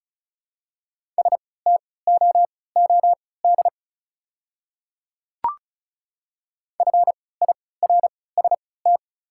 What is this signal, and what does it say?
Morse code, heard as a keyed tone, about 36 wpm, spaced as for 20 wpm, 700 Hz. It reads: STOOD FIRST
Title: 3x Character Spacing - Top 500 Words - 35wpm